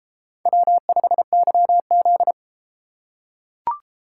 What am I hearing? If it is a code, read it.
Morse code, heard as a keyed tone, about 33 wpm, 700 Hz. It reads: W5YZ